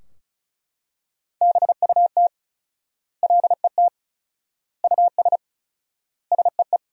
Morse code, heard as a keyed tone, about 34 wpm, 700 Hz. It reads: BUT LET US SEE
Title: 4x Word-Spacing - Sentences from Top 500 Words - 35wpm